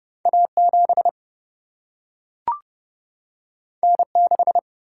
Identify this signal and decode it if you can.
Morse code, heard as a keyed tone, about 30 wpm, 700 Hz. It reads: A7 N6
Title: Single-Letter Single-Number 30wpm